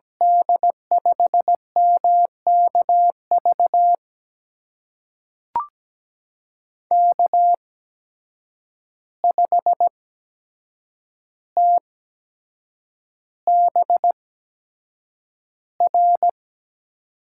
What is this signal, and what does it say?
Morse code, heard as a keyed tone, about 17 wpm, spaced as for 7 wpm, 700 Hz. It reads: D5MKV K5TBR
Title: USA Call Signs - 8x Character Spacing - 17wpm